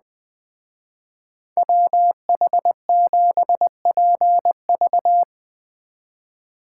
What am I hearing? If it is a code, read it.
Morse code, heard as a keyed tone, about 19 wpm, 700 Hz. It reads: WH7PV
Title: Speed Racing - USA 2x2 Call Signs 17wpm